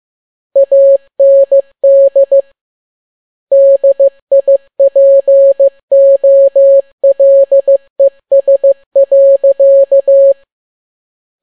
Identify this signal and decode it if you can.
Morse code, heard as a keyed tone, about 15 wpm, 550 Hz. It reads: AND DIPOLES.
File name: n6tt p1 15C13